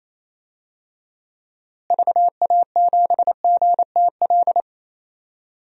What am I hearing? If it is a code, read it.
Morse code, heard as a keyed tone, about 28 wpm, 700 Hz. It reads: VA7GTL
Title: DXCC-100 Call Signs - 6 Characters - Repeated 3x - 28wpm